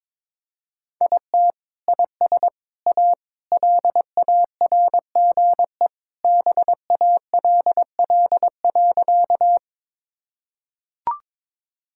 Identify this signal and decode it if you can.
Morse code, heard as a keyed tone, about 22 wpm, 700 Hz. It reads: IT IS A LARGE BALL.
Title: Sentences from Top 1000 Words - Review Part 6 - 22wpm